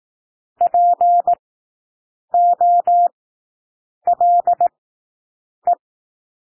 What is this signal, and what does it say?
Morse code, heard as a keyed tone, about 18 wpm, spaced as for 7 wpm, 700 Hz. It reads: POLE
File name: SOUND WORDS 18 wpm